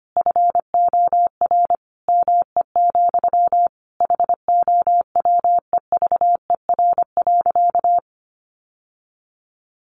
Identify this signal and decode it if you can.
Morse code, heard as a keyed tone, about 25 wpm, 700 Hz. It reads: FOR ME, HOWEVER.
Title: Sentences from Top 600 Words 25wpm